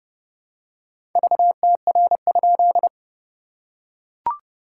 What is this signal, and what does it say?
Morse code, heard as a keyed tone, about 30 wpm, 700 Hz. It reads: VTR?
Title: SOTA QSO Practice 30wpm